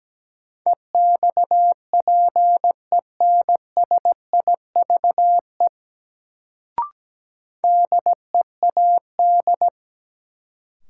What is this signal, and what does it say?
Morse code, heard as a keyed tone, about 17 wpm, 700 Hz. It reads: EXPENSIVE DEAD
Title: Sentences from Top 1500 Words - New Word Drill Alt - 17wpm